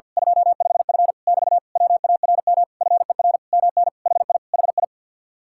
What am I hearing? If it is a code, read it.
Morse code, heard as a keyed tone, about 48 wpm, 700 Hz. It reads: 25F <BT> WARM FER MN HI HI
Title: Speed-Racing Common QSO Phrases - 50wpm